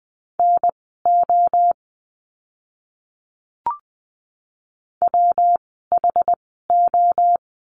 Morse code, heard as a keyed tone, about 20 wpm, spaced as for 14 wpm, 700 Hz. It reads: NO WHO
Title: 244 Lesson 39f - Letter z - Words 20wpm